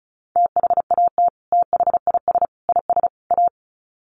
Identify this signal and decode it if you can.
Morse code, heard as a keyed tone, about 34 wpm, 700 Hz. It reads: THAT THIS IS A
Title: Sets of 4 Words 35wpm